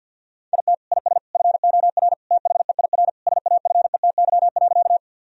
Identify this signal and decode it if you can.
Morse code, heard as a keyed tone, about 51 wpm, 700 Hz. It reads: IT IS FOR THEIR SAFETY.
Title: Sentences from Top 1000 Words - Review Part 4 - 50wpm